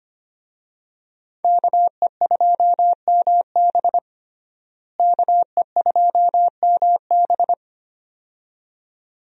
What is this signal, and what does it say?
Morse code, heard as a keyed tone, about 25 wpm, 700 Hz. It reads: KE2MB KE2MB
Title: USA Call Signs - 5 Characters - Repeated 2x - 25wpm